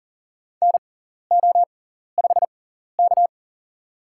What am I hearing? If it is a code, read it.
Morse code, heard as a keyed tone, about 40 wpm, spaced as for 14 wpm, 700 Hz. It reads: NO5K
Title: USA Call Signs - 4 Characters - 6x Character Spacing - 40wpm